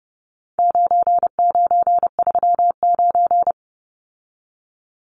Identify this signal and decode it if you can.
Morse code, heard as a keyed tone, about 30 wpm, 700 Hz. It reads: 9939
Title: Numbers - 4-Digit 30wpm